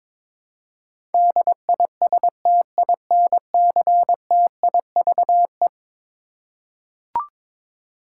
Morse code, heard as a keyed tone, about 22 wpm, 700 Hz. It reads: DISTINCTIVE
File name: practice / 22wpm / Eleven Letter Words - 22wpm